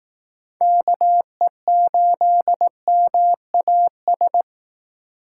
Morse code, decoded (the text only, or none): KE8MAS